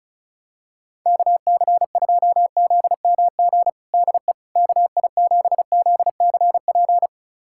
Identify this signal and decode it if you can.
Morse code, heard as a keyed tone, about 34 wpm, 700 Hz. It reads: KC2ZMG DE KI7ZCP
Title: USA Callsign DE Callsign - 6 Characters - 35wpm